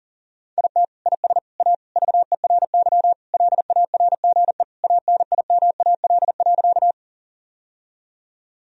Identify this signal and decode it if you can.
Morse code, heard as a keyed tone, about 40 wpm, 700 Hz. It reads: IT IS A VERY LARGE ANIMAL.
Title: Sentences from Top 1000 Words - Review Part 1 - 40wpm